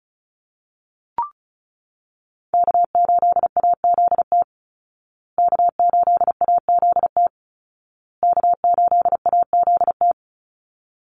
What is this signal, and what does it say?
Morse code, heard as a keyed tone, about 34 wpm, 700 Hz. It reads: K8AZT K8AZT K8AZT
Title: Call Signs Repeated 2x - 35wpm